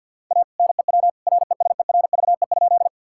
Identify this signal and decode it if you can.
Morse code, heard as a keyed tone, about 50 wpm, 700 Hz. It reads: A NEW RESERVE?
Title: Sentences from Top 1000 Words - Review Part 6 - 50wpm